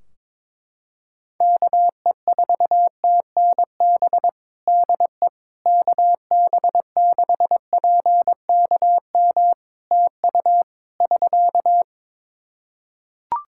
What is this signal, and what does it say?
Morse code, heard as a keyed tone, about 22 wpm, 700 Hz. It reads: KE4TNB DE KB6PKM TU <SK>
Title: Practice QSO - Exchange 4 - Ending - 22wpm